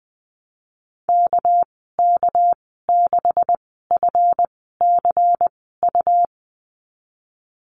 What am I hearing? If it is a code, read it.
Morse code, heard as a keyed tone, about 20 wpm, spaced as for 15 wpm, 700 Hz. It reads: KK6FCU
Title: 207 Lesson 33g - Letter j - Call Signs 20wpm